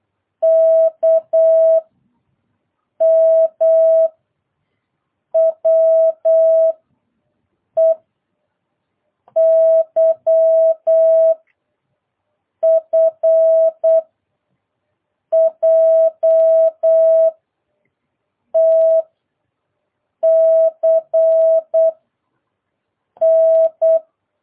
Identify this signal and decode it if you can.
Morse code, heard as a keyed tone, about 8 wpm, 650 Hz. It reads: K M W E Y F J T C N